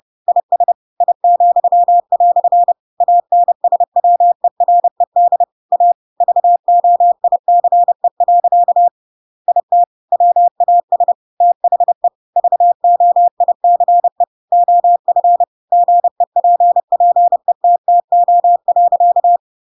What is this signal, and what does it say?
Morse code, heard as a keyed tone, about 30 wpm, 700 Hz. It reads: IS I," ANSWERED A VOICE. IT WAS THE VOICE OF GEPPETTO.
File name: Pinocchio / Pinocchio ch6 30 30 ik1hge